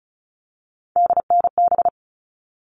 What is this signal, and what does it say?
Morse code, heard as a keyed tone, about 34 wpm, 700 Hz. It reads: DNB
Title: All Cut Numbers - 3-Digit 35wpm